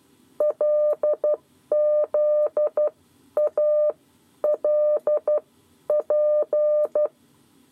Morse code, heard as a keyed tone, about 12 wpm, spaced as for 10 wpm, 600 Hz. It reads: LZALP
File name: 7 Lezione cw 2